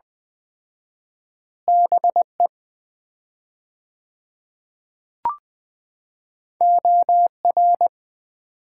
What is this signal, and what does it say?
Morse code, heard as a keyed tone, about 20 wpm, 700 Hz. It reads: BE OR
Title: 4x Word-Spacing - Sets of 2 Words - 20wpm